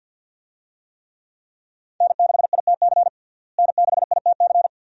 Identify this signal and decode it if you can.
Morse code, heard as a keyed tone, about 51 wpm, 700 Hz. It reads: N6ITC N6ITC
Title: USA Call Signs - 5 Characters - Repeated 2x - 50wpm